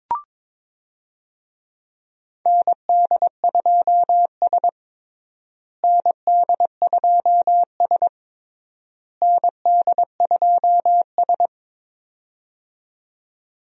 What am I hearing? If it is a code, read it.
Morse code, heard as a keyed tone, about 22 wpm, 700 Hz. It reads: ND2S ND2S ND2S
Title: USA Call Signs - 4 Characters - Repeated 3x - 22wpm